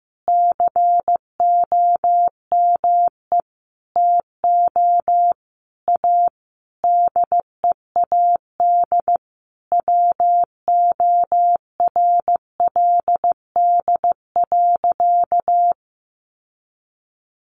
Morse code, heard as a keyed tone, about 15 wpm, 700 Hz. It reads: COME TO A DEAD WORLD.
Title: Sentences from Top 1500 Words 15wpm